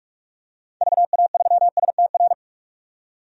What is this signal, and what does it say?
Morse code, heard as a keyed tone, about 46 wpm, 700 Hz. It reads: VA3STR
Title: DXCC-100 Call Signs - 6 Characters - 45wpm